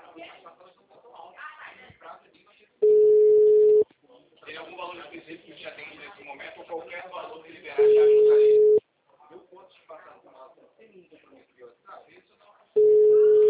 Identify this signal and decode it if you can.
Ringback tone (Europe, 425 Hz), with speech.